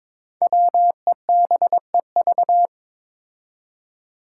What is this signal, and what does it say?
Morse code, heard as a keyed tone, about 22 wpm, 700 Hz. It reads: WEBEV
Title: Five Challenging Letter-Number Combinations 22wpm